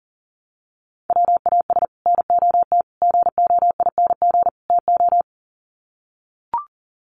Morse code, heard as a keyed tone, about 40 wpm, 700 Hz. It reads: WAS NOT GOING TO
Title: Sets of 4 Words 40wpm